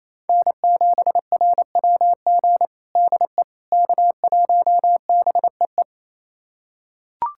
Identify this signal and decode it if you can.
Morse code, heard as a keyed tone, about 28 wpm, 700 Hz. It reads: N7RWG DE K1BEE
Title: USA Callsign DE Callsign - 5 Characters - 28wpm